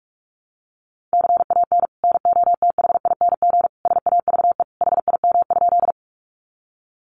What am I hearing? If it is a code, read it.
Morse code, heard as a keyed tone, about 44 wpm, 700 Hz. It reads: CAN NOTHING SAVE HIM?
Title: Sentences from Top 1100 Words 45wpm